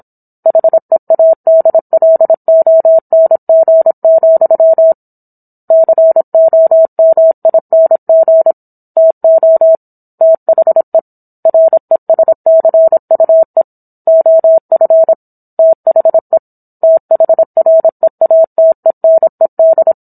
Morse code, heard as a keyed tone, about 27 wpm, 650 Hz. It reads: HEADLONG, COMING TO THE RESCUE OF THE THREATENED